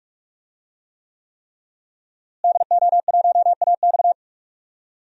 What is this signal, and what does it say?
Morse code, heard as a keyed tone, about 41 wpm, 700 Hz. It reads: DO1AX
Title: DXCC-100 Call Signs - 5 Characters - Repeated 3x - 45wpm